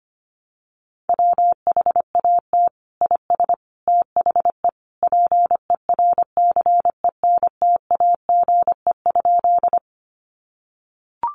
Morse code, heard as a keyed tone, about 25 wpm, 700 Hz. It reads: WHAT IS THE PERCENTAGE?
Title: Sentences from Top 1300 Words 25wpm